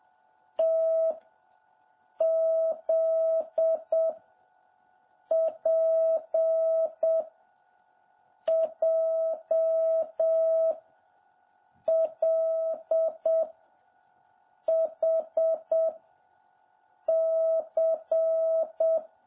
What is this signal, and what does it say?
Morse code, heard as a keyed tone, about 7 wpm, 650 Hz. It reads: T Z P J L H C
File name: jh8cbh / 2021cw